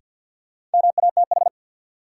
Morse code, heard as a keyed tone, about 48 wpm, 700 Hz. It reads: MATH
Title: Sentences from Top 1000 Words - Review Part 6 - New Word Drill - 50wpm